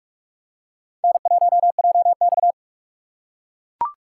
Morse code, heard as a keyed tone, about 41 wpm, 700 Hz. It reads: N1JX
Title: USA Call Signs - 2x Character Spacing - 45wpm